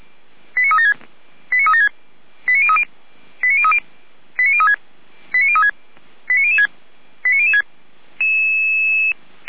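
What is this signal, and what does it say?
Five-tone selective call (ZVEI), heard as quick runs of five tones, tones of about 70 ms. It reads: 89376 89376 89030 89030 89036 89036 89006 89006
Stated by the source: ZVEI five-tone selective call received by SDR, fire brigade feuerwehr-orlamuende.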